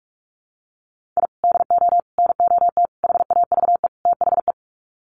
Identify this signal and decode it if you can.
Morse code, heard as a keyed tone, about 44 wpm, 700 Hz. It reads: I DO NOT HAVE THE
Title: Sets of 5 Words 45wpm